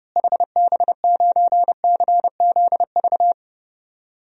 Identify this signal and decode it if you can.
Morse code, heard as a keyed tone, about 32 wpm, 700 Hz. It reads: HB9CZV
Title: DXCC-100 Call Signs - 6 Characters - Repeated 3x - 30wpm